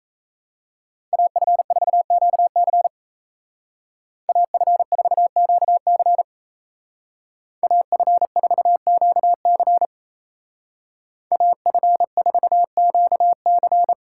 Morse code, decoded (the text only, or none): AF4QC AF4QC AF4QC AF4QC